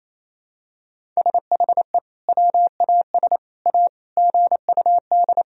Morse code, heard as a keyed tone, about 28 wpm, 700 Hz. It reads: SHE WAS A GUD
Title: Sentences from Top 400 Words and Abbreviations 28wpm